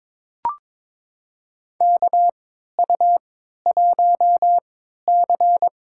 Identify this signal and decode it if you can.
Morse code, heard as a keyed tone, about 21 wpm, spaced as for 12 wpm, 700 Hz. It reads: KU1C
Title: USA Call Signs - 4 Characters - 3x Character Spacing - 22wpm